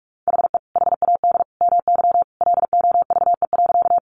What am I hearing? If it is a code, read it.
Morse code, heard as a keyed tone, about 44 wpm, 700 Hz. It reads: HE HAD MY LOVE.
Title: Sentences from Top 400 Words 45wpm